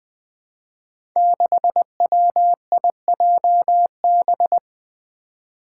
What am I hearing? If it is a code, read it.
Morse code, heard as a keyed tone, about 20 wpm, 700 Hz. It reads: 6WIJB
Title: Five Challenging Letter-Number Combinations 20wpm